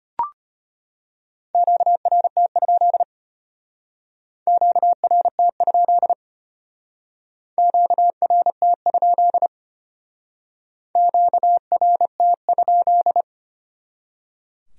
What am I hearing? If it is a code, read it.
Morse code, heard as a keyed tone, about 32 wpm, 700 Hz. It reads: QRT? QRT? QRT? QRT?